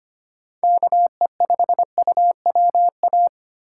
Morse code, heard as a keyed tone, about 25 wpm, 700 Hz. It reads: KE5UWA
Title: USA Call Signs - 6 Characters - 4x Character Spacing - 25wpm